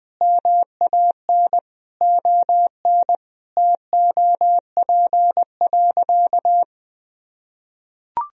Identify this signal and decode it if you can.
Morse code, heard as a keyed tone, about 20 wpm, 700 Hz. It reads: MAN ON TOP.
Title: Sentences from Top 1000 Words - Review Part 2 - 20wpm